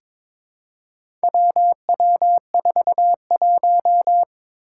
Morse code, heard as a keyed tone, about 21 wpm, 700 Hz. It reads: WW41
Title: Four Challenging Letter-Number Combinations 22wpm